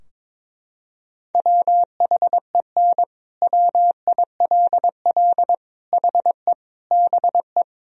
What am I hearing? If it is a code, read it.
Morse code, heard as a keyed tone, about 22 wpm, 700 Hz. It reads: WHEN WILL HE BE
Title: Sentences from Top 1000 Words - Review Part 4 - 22wpm